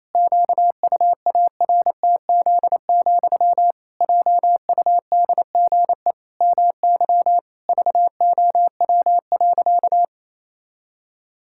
Morse code, heard as a keyed tone, about 28 wpm, 700 Hz. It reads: QUARTZ, JUDGE MY VOW.
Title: Pangram Challenge v2 - 28wpm